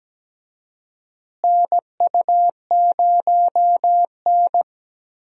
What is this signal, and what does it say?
Morse code, heard as a keyed tone, about 17 wpm, 700 Hz. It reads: NU0N
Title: USA Call Signs - 4 Characters - Repeated 2x - 17wpm